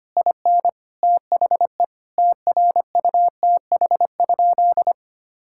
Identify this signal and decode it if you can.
Morse code, heard as a keyed tone, about 25 wpm, 700 Hz. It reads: IN THE TRUTH?
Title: Sentences from Top 1000 Words - Review Part 6 - 25wpm